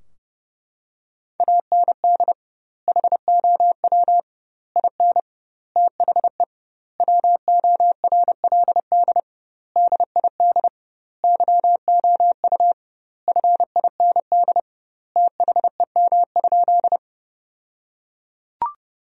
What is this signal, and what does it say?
Morse code, heard as a keyed tone, about 30 wpm, 700 Hz. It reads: AND HOW IN THE WORLD DID YOU FIND THEM?
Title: 2x Word-Spacing - Sentences from Top 200 Words - 30wpm